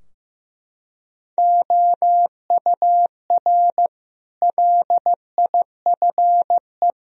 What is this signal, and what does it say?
Morse code, heard as a keyed tone, about 15 wpm, 700 Hz. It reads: OUR LIFE
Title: Sentences from Top 1000 Words - Review Part 3 - 15wpm